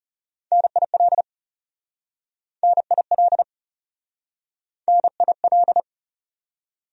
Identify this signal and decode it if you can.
Morse code, heard as a keyed tone, about 40 wpm, 700 Hz. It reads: NIL NIL NIL